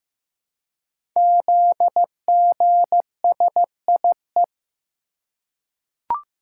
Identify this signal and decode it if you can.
Morse code, heard as a keyed tone, about 15 wpm, 700 Hz. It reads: ZGSIE
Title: Code Groups - 5 Letters - 15wpm